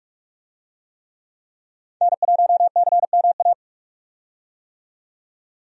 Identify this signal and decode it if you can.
Morse code, heard as a keyed tone, about 45 wpm, 700 Hz. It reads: N1CMA
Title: USA Call Signs - 5 Characters - 45wpm